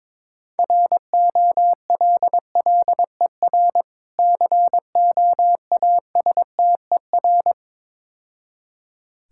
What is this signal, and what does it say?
Morse code, heard as a keyed tone, about 23 wpm, 700 Hz. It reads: ROLLER COASTER
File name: Easy Words - Speed-Racing with Hints - 22wpm